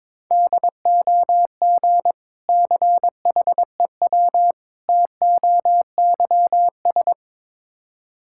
Morse code, heard as a keyed tone, about 21 wpm, 700 Hz. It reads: DOG CHEW TOYS
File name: Difficult Words with Hints - 22wpm